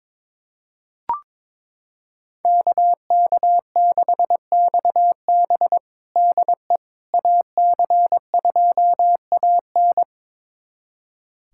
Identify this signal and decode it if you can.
Morse code, heard as a keyed tone, about 21 wpm, 700 Hz. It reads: KK6XB DE AC2AN
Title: USA Callsign DE Callsign - 5 Characters - 22wpm